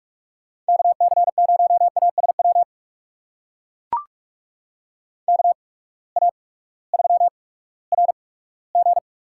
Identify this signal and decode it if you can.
Morse code, heard as a keyed tone, about 44 wpm, spaced as for 20 wpm, 700 Hz. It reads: KC0ASW KA3RG
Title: USA Call Signs - 8x Character Spacing - 45wpm